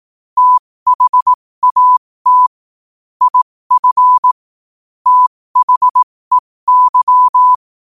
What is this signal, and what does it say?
Morse code, heard as a keyed tone, about 18 wpm, spaced as for 15 wpm, 1000 Hz. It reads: THAT IF THEY
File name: 15wpm